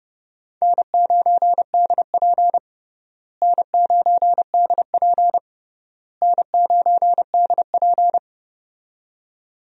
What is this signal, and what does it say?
Morse code, heard as a keyed tone, about 30 wpm, 700 Hz. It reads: N9DP N9DP N9DP
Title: USA Call Signs - 4 Characters - Repeated 3x - 30wpm